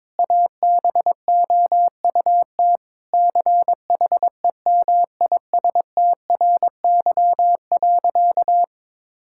Morse code, heard as a keyed tone, about 22 wpm, 700 Hz. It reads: ABOUT CHEMISTRY.
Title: Sentences from Top 1000 Words - Review Part 6 - 22wpm